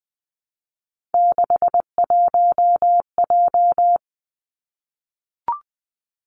Morse code, heard as a keyed tone, about 20 wpm, 700 Hz. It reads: 61J
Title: Three Challenging Letter-Number Combinations 20wpm